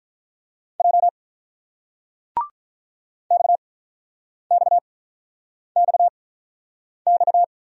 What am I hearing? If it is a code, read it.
Morse code, heard as a keyed tone, about 46 wpm, spaced as for 9 wpm, 690 Hz. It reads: J XXXX